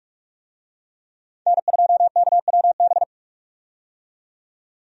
Morse code, heard as a keyed tone, about 46 wpm, 700 Hz. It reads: N2KWB
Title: USA Call Signs - 5 Characters - 45wpm